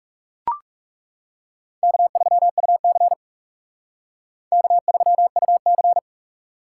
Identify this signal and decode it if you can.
Morse code, heard as a keyed tone, about 37 wpm, 700 Hz. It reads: K3UC K3UC
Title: Speed Racing - USA 1x2 Call Signs 30wpm